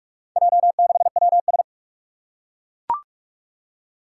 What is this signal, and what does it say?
Morse code, heard as a keyed tone, about 41 wpm, 700 Hz. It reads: J6WS
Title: Four Challenging Letter-Number Combinations 45wpm